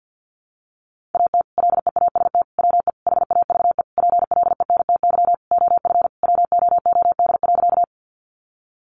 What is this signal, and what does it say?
Morse code, heard as a keyed tone, about 50 wpm, 700 Hz. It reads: AT LEAST WE HAVE PLENTY OF WOOD.
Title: Sentences from Top 1300 Words 50wpm